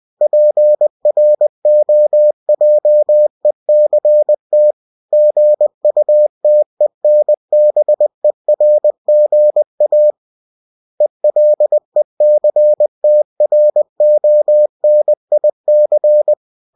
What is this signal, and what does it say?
Morse code, heard as a keyed tone, about 20 wpm, 600 Hz. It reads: PROJECT GUTENBERGA ELECTRONIC
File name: cw-books / winnie-the-pooh / winnie-20wpm-page-0113